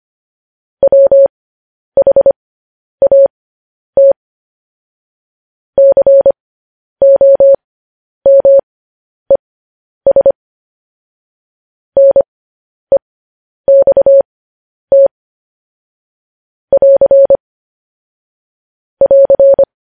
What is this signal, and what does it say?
Morse code, heard as a keyed tone, about 25 wpm, spaced as for 10 wpm, 550 Hz. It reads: WHAT COMES NEXT <AR> <AR>
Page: news headlines in morse code at 10 wpm